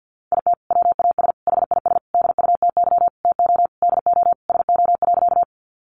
Morse code, heard as a keyed tone, about 50 wpm, 700 Hz. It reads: IT WAS HIS DUTY TO DO SO.
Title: Sentences from Top 1500 Words 50wpm